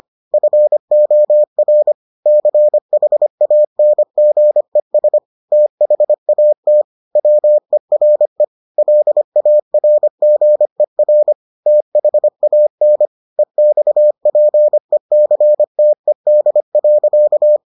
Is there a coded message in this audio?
Yes: Morse code from 0.3 to 17.6 s, heard as a keyed tone, about 25 wpm, 600 Hz. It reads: FOR CHANGES THAT WERE LARGER THAN EXPECTED.